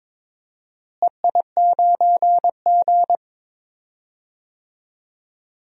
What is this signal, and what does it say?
Morse code, heard as a keyed tone, about 22 wpm, 700 Hz. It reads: EI9G
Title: DXCC-100 Call Signs - 4 Characters - Repeated 2x - 22wpm